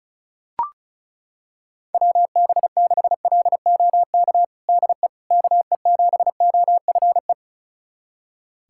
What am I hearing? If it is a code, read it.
Morse code, heard as a keyed tone, about 34 wpm, 700 Hz. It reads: WB6LOK DE KE7OFE
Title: USA Callsign DE Callsign - 6 Characters - 35wpm